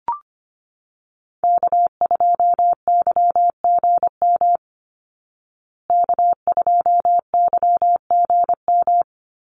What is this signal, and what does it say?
Morse code, heard as a keyed tone, about 25 wpm, 700 Hz. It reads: K2YGM K2YGM
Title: Call Signs Repeated 2x - 25wpm